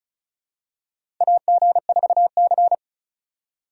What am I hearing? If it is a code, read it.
Morse code, heard as a keyed tone, about 36 wpm, 700 Hz. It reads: AG4C
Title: USA Call Signs - 2x Character Spacing - 35wpm